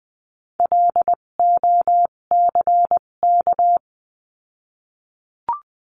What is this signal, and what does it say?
Morse code, heard as a keyed tone, about 20 wpm, spaced as for 17 wpm, 700 Hz. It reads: LOCK
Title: 257 Lesson 42b - Words 14-20wpm